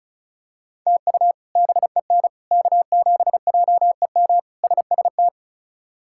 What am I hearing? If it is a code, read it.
Morse code, heard as a keyed tone, about 36 wpm, 700 Hz. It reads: TU BEN K7JEM SST
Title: K1USN SST QSO Practice - 35wpm